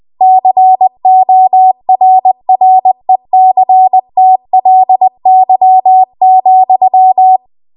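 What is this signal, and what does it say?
Morse code, heard as a keyed tone, about 20 wpm, 750 Hz. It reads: CORRECTLY,
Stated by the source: Morse code practice at 20 WPM.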